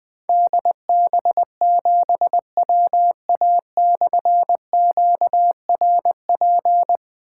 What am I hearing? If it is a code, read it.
Morse code, heard as a keyed tone, about 20 wpm, 700 Hz. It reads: DB7WA/QRP
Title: Difficult Call Signs 20wpm